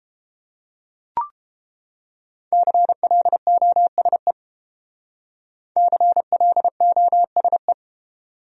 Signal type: Morse code

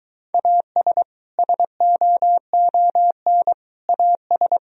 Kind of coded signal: Morse code

